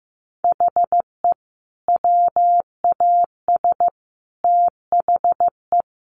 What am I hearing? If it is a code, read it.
Morse code, heard as a keyed tone, about 15 wpm, 700 Hz. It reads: HE WAS THE
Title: Sets of 5 Words Encore2 15wpm